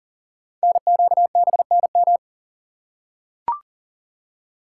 Morse code, heard as a keyed tone, about 39 wpm, 690 Hz. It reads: NQBNM
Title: Code Groups - 5 Letters - 40wpm